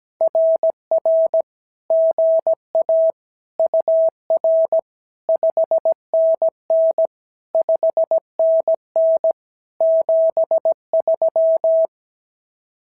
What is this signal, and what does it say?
Morse code, heard as a keyed tone, about 17 wpm, 650 Hz. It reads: RR GA UR 5NN 5NN 73
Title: SOTA QSO Practice 17wpm